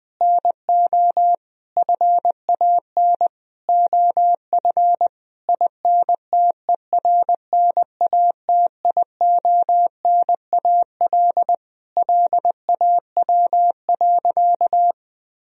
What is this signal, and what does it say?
Morse code, heard as a keyed tone, about 20 wpm, 700 Hz. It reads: NO FAN OF INTERNATIONAL LAW.
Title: Sentences from Top 1000 Words - Review Part 4 - 20wpm